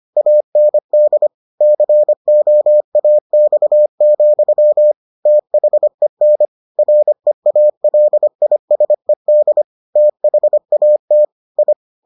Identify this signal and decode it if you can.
Morse code, heard as a keyed tone, about 25 wpm, 600 Hz. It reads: AND COAX, THEN REALISED THAT I